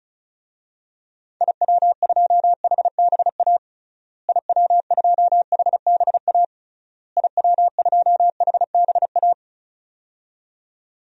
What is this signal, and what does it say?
Morse code, heard as a keyed tone, about 36 wpm, 700 Hz. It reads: IW2HBA IW2HBA IW2HBA
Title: DXCC-100 Call Signs - 6 Characters - Repeated 3x - 35wpm